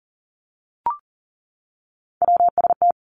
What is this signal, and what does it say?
Morse code, heard as a keyed tone, about 38 wpm, 700 Hz. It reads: WST
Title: Three Challenging Letter-Number Combinations 40wpm